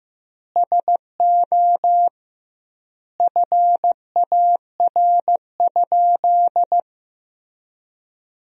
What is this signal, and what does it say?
Morse code, heard as a keyed tone, about 15 wpm, 700 Hz. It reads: SO FAR?
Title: 2x Word-Spacing - Sentences from Top 300 Words - 15wpm